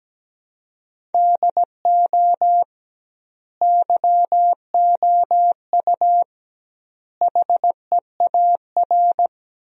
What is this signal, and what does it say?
Morse code, heard as a keyed tone, about 17 wpm, 700 Hz. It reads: DO YOU HEAR
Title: 2x Word-Spacing - Sentences from Top 300 Words - 17wpm